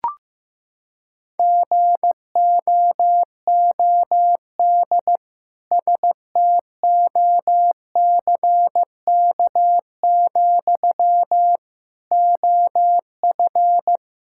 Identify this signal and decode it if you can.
Morse code, heard as a keyed tone, about 15 wpm, 700 Hz. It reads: GOOD STOCK, OF
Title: Sentences from Top 1000 Words - Review Part 5 - 15wpm